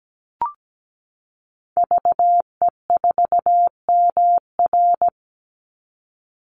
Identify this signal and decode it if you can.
Morse code, heard as a keyed tone, about 17 wpm, 700 Hz. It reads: VE4MR
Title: Call Signs Repeated 3x - 17wpm